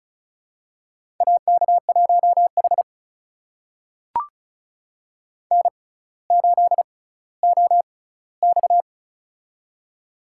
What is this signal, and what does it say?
Morse code, heard as a keyed tone, about 36 wpm, spaced as for 18 wpm, 700 Hz. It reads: AK1H N8OX
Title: USA Call Signs - 4 Characters - 6x Character Spacing - 35wpm